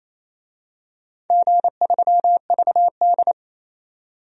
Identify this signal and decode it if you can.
Morse code, heard as a keyed tone, about 28 wpm, 700 Hz. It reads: G3VD
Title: DXCC-100 Call Signs - 4 Characters - 28wpm